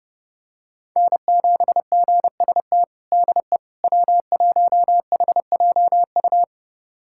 Morse code, heard as a keyed tone, about 30 wpm, 700 Hz. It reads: N7GST DE W1HJU
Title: USA Callsign DE Callsign - 5 Characters - 30wpm